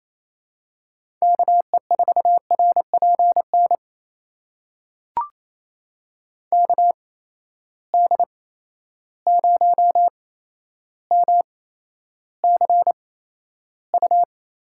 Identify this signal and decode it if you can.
Morse code, heard as a keyed tone, about 28 wpm, spaced as for 12 wpm, 700 Hz. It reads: KE4RPN KD0MCU
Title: USA Call Signs - 6 Characters - 8x Character Spacing - 28wpm